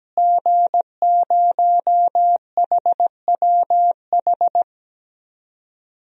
Morse code, heard as a keyed tone, about 17 wpm, 700 Hz. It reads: G0HWH
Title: DXCC-100 Call Signs - 5 Characters - Repeated 3x - 17wpm